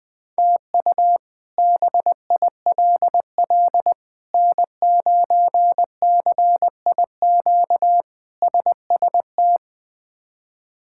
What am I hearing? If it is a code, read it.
Morse code, heard as a keyed tone, about 21 wpm, 700 Hz. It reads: TU BILL N9CIQ SST
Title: K1USN SST QSO Practice - 20wpm